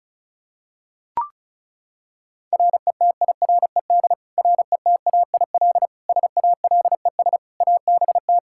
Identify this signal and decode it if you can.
Morse code, heard as a keyed tone, about 34 wpm, 700 Hz. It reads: RETIRED RETAIL SALES ABT
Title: Practice QSO - Exchange 3 - More Info - 35wpm